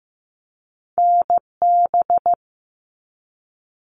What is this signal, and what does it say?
Morse code, heard as a keyed tone, about 15 wpm, 700 Hz. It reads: NB
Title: Canadian Provinces and Territories 15wpm